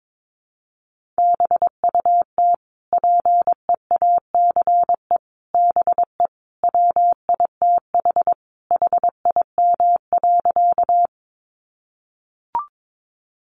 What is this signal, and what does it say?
Morse code, heard as a keyed tone, about 22 wpm, 700 Hz. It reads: BUT PEACE BE WITH HIM.
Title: Sentences from Top 1500 Words 22wpm